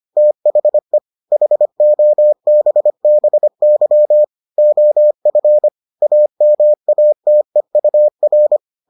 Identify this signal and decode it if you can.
Morse code, heard as a keyed tone, about 25 wpm, 600 Hz. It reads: THE HOBBY OF AMATEUR